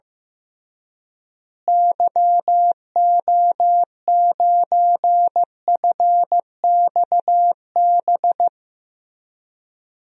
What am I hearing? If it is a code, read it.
Morse code, heard as a keyed tone, about 15 wpm, 700 Hz. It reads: YO9FXB
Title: DXCC-100 Call Signs - 6 Characters - Repeated 3x - 15wpm